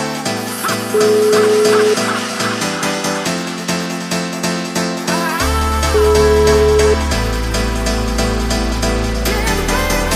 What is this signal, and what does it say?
Ringback tone (Europe, 425 Hz), with noise.